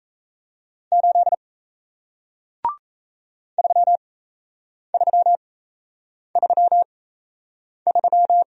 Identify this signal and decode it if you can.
Morse code, heard as a keyed tone, about 38 wpm, spaced as for 8 wpm, 700 Hz. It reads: 8 3333